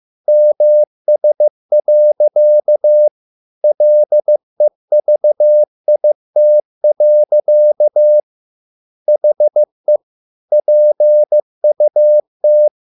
Morse code, heard as a keyed tone, about 15 wpm, 600 Hz. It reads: MS. LEVIT. HE PUT